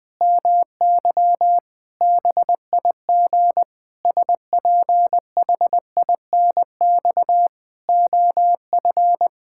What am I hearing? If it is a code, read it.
Morse code, heard as a keyed tone, about 20 wpm, 700 Hz. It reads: MY BIG SPHINX OF
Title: Pangram Challenge v2 - 20wpm